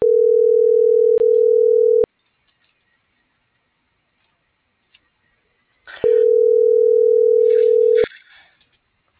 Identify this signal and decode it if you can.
Ringback tone (North America), 440 + 480 Hz.